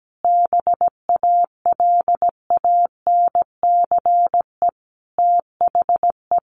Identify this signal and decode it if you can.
Morse code, heard as a keyed tone, about 17 wpm, 700 Hz. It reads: BALANCE THE
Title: Sentences from Top 1200 Words 17wpm